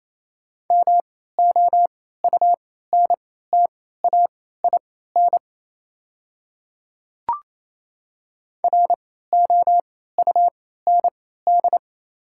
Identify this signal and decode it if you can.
Morse code, heard as a keyed tone, about 27 wpm, spaced as for 16 wpm, 700 Hz. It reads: MOUNTAIN ROUND
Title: 3x Character Spacing - Top 300 Words - 28wpm